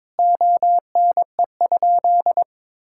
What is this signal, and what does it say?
Morse code, heard as a keyed tone, about 21 wpm, 700 Hz. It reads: ONE?
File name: practice / 22wpm / Sentences with 3-Word Phrases - 22wpm